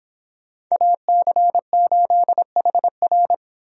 Morse code, heard as a keyed tone, about 26 wpm, 700 Hz. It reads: AC8HR